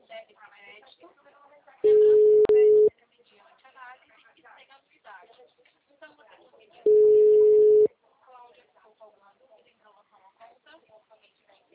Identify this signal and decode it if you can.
Ringback tone (Europe, 425 Hz), with speech.